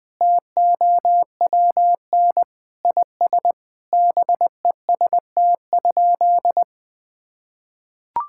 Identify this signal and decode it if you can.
Morse code, heard as a keyed tone, about 20 wpm, 700 Hz. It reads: TOWN IS BEST?
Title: Sentences from Top 1000 Words - Review Part 5 - 20wpm